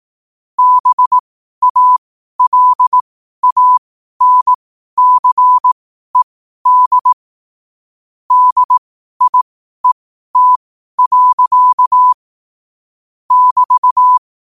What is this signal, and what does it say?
Morse code, heard as a keyed tone, about 18 wpm, spaced as for 12 wpm, 1000 Hz. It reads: BALANCED DIET. <BT>
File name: qotdfiles / 12wpm